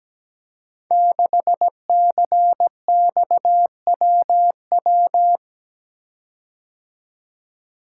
Morse code, heard as a keyed tone, about 17 wpm, 700 Hz. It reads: 6CXWW
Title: Code Groups - 5 Letter:Numbers - 17wpm